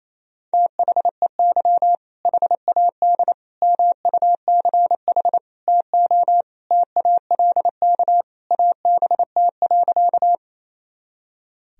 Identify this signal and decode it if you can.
Morse code, heard as a keyed tone, about 28 wpm, 700 Hz. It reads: THEY HAD MUCH TO TALK ABT.
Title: Sentences from Top 400 Words and Abbreviations 28wpm